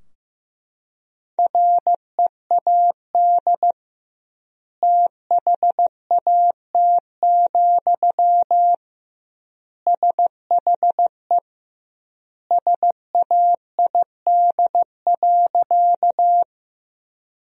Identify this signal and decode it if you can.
Morse code, heard as a keyed tone, about 15 wpm, 700 Hz. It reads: READ THAT, SHE SAID.
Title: 2x Word-Spacing - Sentences from Top 200 Words - 15wpm